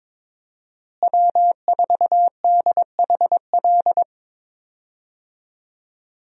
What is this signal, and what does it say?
Morse code, heard as a keyed tone, about 22 wpm, 700 Hz. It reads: W4DHL
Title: USA Call Signs - 5 Characters - Repeated 2x - 22wpm